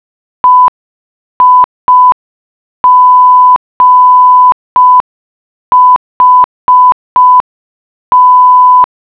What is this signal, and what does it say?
Morse code, heard as a keyed tone, about 5 wpm, 1000 Hz. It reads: EIGHT